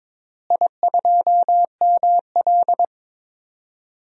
Morse code, heard as a keyed tone, about 22 wpm, 700 Hz. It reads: I2ML
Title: DXCC-100 Call Signs - 4 Characters - Repeated 2x - 22wpm